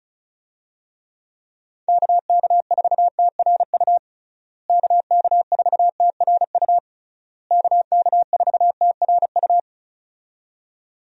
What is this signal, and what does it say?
Morse code, heard as a keyed tone, about 36 wpm, 700 Hz. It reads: KK4TRU KK4TRU KK4TRU
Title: USA Call Signs - 6 Characters - Repeated 3x - 35wpm